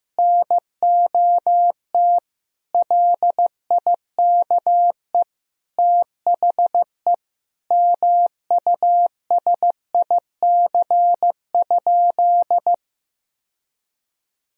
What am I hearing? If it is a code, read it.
Morse code, heard as a keyed tone, about 16 wpm, 700 Hz. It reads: NOT LIKE THE MUSIC?
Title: Sentences from Top 1000 Words - Review Part 1 - 15wpm